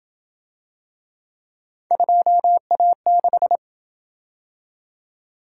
Morse code, heard as a keyed tone, about 27 wpm, 700 Hz. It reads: 2A6